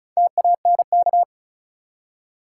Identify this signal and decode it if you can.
Morse code, heard as a keyed tone, about 36 wpm, 700 Hz. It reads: TANK
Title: Four Letter Words - 35wpm